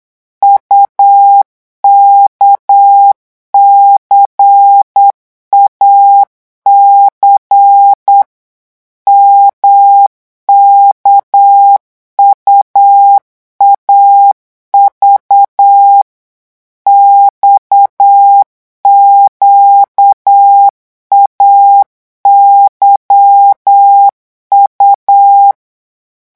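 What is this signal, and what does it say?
Morse code, heard as a keyed tone, about 8 wpm, 800 Hz. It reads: UKCAC MKUAV XQAYU
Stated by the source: Morse code revision lesson at 8 words per minute, from a French amateur radio course.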